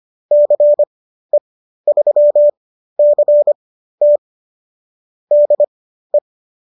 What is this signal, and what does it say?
Morse code, heard as a keyed tone, about 25 wpm, spaced as for 13 wpm, 590 Hz. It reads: CE3CT DE